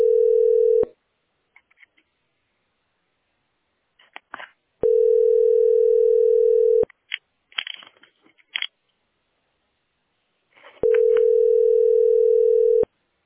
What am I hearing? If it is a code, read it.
Ringback tone (North America), 440 + 480 Hz.